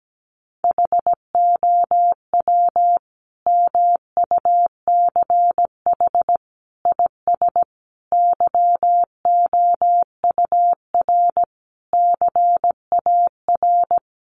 Morse code, heard as a keyed tone, about 17 wpm, 700 Hz. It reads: HOW MUCH IS YOUR CAR
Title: Sentences from Top 1100 Words 17wpm